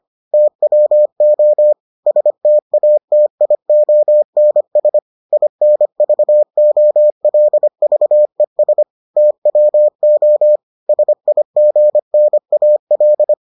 Morse code, heard as a keyed tone, about 25 wpm, 600 Hz. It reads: TWO STATIONS INVOLVES TWO SIGNAL